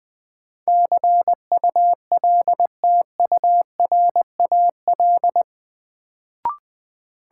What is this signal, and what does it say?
Morse code, heard as a keyed tone, about 20 wpm, 700 Hz. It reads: CULTURAL
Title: Sentences from Top 1500 Words - New Word Drill - 20wpm